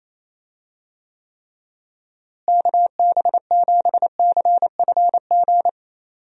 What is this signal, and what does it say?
Morse code, heard as a keyed tone, about 28 wpm, 700 Hz. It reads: KB7CFG
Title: USA Call Signs - 6 Characters - Repeated 3x - 28wpm